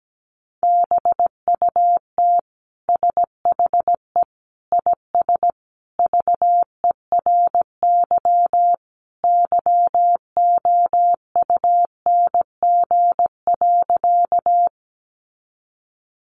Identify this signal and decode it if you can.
Morse code, heard as a keyed tone, about 17 wpm, 700 Hz. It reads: BUT SHE IS VERY YOUNG.
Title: Sentences from Top 400 Words 17wpm